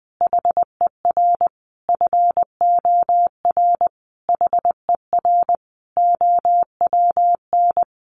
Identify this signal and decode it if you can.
Morse code, heard as a keyed tone, about 20 wpm, 700 Hz. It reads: HER FOR HER OWN
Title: Sentences from Top 1300 Words 20wpm